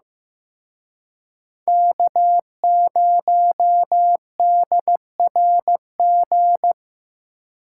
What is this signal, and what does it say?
Morse code, heard as a keyed tone, about 18 wpm, 700 Hz. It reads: K0DRG
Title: Speed Racing - USA 1x3 Call Signs 15wpm